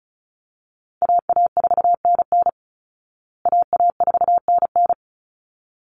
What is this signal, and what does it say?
Morse code, heard as a keyed tone, about 34 wpm, 700 Hz. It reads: AA4NN AA4NN
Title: Call Signs Repeated 3x - 35wpm